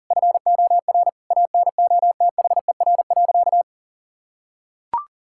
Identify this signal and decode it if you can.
Morse code, heard as a keyed tone, about 40 wpm, 700 Hz. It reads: FOR ANOTHER.
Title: Sentences from Top 1000 Words - Review Part 6 - 40wpm